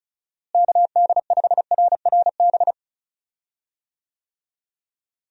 Morse code, heard as a keyed tone, about 34 wpm, 700 Hz. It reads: KD5RRB
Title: USA Call Signs - 6 Characters - Repeated 2x - 35wpm